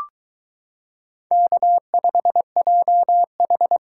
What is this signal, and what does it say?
Morse code, heard as a keyed tone, about 22 wpm, 700 Hz. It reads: K5JH